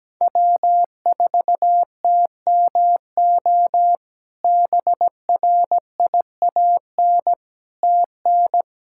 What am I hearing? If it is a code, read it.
Morse code, heard as a keyed tone, about 17 wpm, 700 Hz. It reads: W4TMO BRIAN TN
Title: K1USN SST QSO Practice - 17wpm